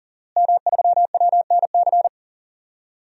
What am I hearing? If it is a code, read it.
Morse code, heard as a keyed tone, about 40 wpm, 700 Hz. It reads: M3WNC
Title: DXCC-100 Call Signs - 5 Characters - Repeated 2x - 40wpm